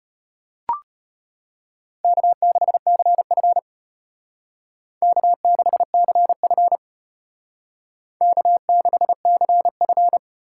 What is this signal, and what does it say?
Morse code, heard as a keyed tone, about 33 wpm, 700 Hz. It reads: K6CF K6CF K6CF